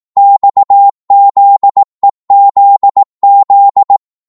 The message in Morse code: XZEZZ